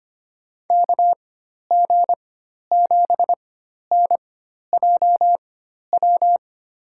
Morse code, heard as a keyed tone, about 25 wpm, spaced as for 12 wpm, 700 Hz. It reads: KG7NJW
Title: USA Call Signs - 6 Characters - 4x Character Spacing - 25wpm